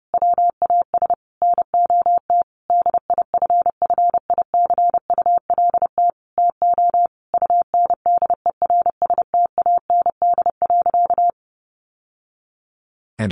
Morse code, heard as a keyed tone, about 30 wpm, 700 Hz. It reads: WAS NOT DIFFICULT TO UNDERSTAND.
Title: Sentences from Top 1100 Words 30wpm